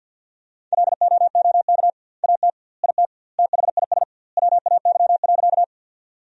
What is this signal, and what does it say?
Morse code, heard as a keyed tone, about 50 wpm, 700 Hz. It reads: LOOK AT IT THIS WAY.
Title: 2x Word-Spacing - Sentences from Top 100 Words - 50wpm